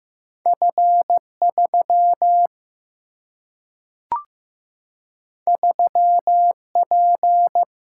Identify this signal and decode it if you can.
Morse code, heard as a keyed tone, about 15 wpm, 700 Hz. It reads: F 3 3 P
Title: Single-Letter Single-Number 15wpm